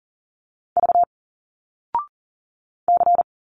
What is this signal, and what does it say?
Morse code, heard as a keyed tone, about 40 wpm, 700 Hz. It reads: V C
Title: Single Letters 40wpm